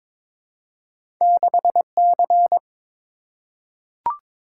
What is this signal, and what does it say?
Morse code, heard as a keyed tone, about 22 wpm, 700 Hz. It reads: 6C
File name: practice / 22wpm / Two Letter-Number Combinations - 22wpm